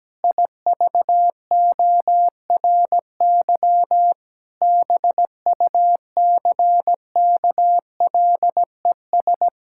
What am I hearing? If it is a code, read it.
Morse code, heard as a keyed tone, about 17 wpm, 700 Hz. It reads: IVORY BUCKLES
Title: Pangram Challenge v2 - 17wpm